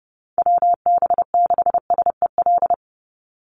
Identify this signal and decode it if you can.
Morse code, heard as a keyed tone, about 30 wpm, 700 Hz. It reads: WB6SEL
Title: Call Signs Repeated 3x - 30wpm